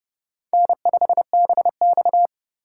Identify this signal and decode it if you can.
Morse code, heard as a keyed tone, about 32 wpm, 700 Hz. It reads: N5BX
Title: USA Call Signs - 4 Characters - 2x Character Spacing - 30wpm